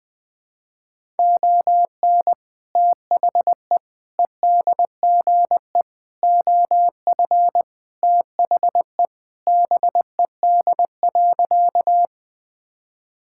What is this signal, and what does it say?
Morse code, heard as a keyed tone, about 21 wpm, 700 Hz. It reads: ON THE EDGE OF THE BED.
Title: Sentences with 3-Word Phrases - 20wpm